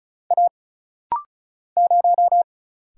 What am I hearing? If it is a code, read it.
Morse code, heard as a keyed tone, about 36 wpm, 700 Hz. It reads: A 0
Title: Single Letter-Number - Rapid-Fire 35wpm